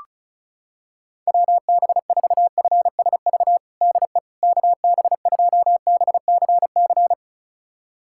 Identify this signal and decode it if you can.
Morse code, heard as a keyed tone, about 36 wpm, 700 Hz. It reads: WB4FSV DE KB2BCC
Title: USA Callsign DE Callsign - 6 Characters - 35wpm